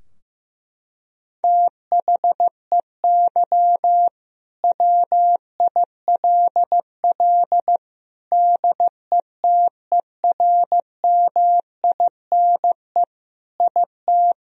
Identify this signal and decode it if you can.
Morse code, heard as a keyed tone, about 15 wpm, 700 Hz. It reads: THEY WILL DETERMINE IT
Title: Sentences from Top 1000 Words - Review Part 4 - 15wpm